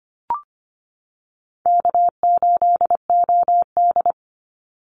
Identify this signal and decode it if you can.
Morse code, heard as a keyed tone, about 25 wpm, 700 Hz. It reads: K8OD
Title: Easy Call Signs - 25wpm